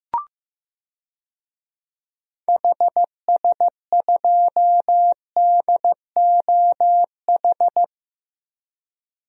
Morse code, heard as a keyed tone, about 15 wpm, 700 Hz. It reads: HS2DOH
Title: DXCC-100 Call Signs - 6 Characters - Repeated 2x - 15wpm